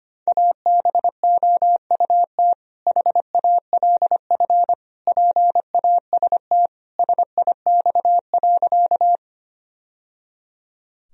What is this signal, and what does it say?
Morse code, heard as a keyed tone, about 25 wpm, 700 Hz. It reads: ABOUT HALF PAST SIX.
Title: Sentences from Top 1000 Words - Review Part 3 - 25wpm